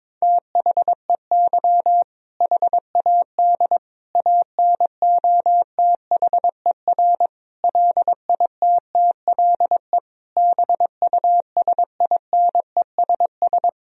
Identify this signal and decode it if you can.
Morse code, heard as a keyed tone, about 22 wpm, 700 Hz. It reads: THEY HAD ANOTHER LITTLE BUSINESS
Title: Sentences from Top 1000 Words - Review Part 1 - 22wpm